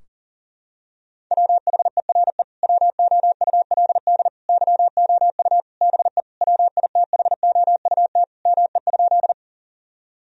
Morse code, heard as a keyed tone, about 40 wpm, 700 Hz. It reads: WHERE WOULD YOU BE WITHOUT ME?
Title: Sentences from Top 1000 Words - Review Part 3 - 40wpm